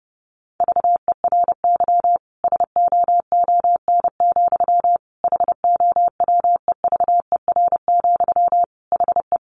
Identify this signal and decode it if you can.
Morse code, heard as a keyed tone, about 30 wpm, 700 Hz. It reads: VERY SOON, HOWEVER, HE
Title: Sentences from Top 1400 Words 30wpm